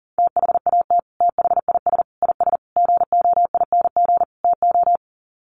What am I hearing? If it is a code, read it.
Morse code, heard as a keyed tone, about 40 wpm, 700 Hz. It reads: THAT THIS IS GOING TO
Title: Sets of 5 Words Encore2 40wpm